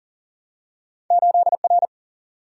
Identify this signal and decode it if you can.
Morse code, heard as a keyed tone, about 40 wpm, 700 Hz. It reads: 8R
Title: Single-Letter Single-Number 40wpm